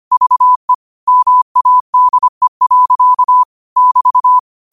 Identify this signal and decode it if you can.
Morse code, heard as a keyed tone, about 25 wpm, 1000 Hz. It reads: VE MADE. <BT>
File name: qotdfiles / 25wpm